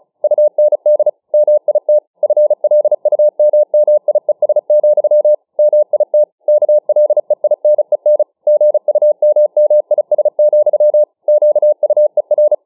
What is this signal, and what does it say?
Morse code, heard as a keyed tone, about 35 wpm, 600 Hz. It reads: UND MIT FLUMMIES, MIT KLEINEN GUMMIS, QUER
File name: dk5ke / Morsemogeln 35WpM